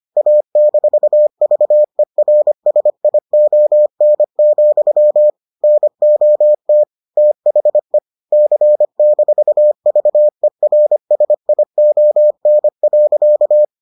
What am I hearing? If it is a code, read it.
Morse code, heard as a keyed tone, about 25 wpm, 600 Hz. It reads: A-VERSION, NOT THE C-VERSION.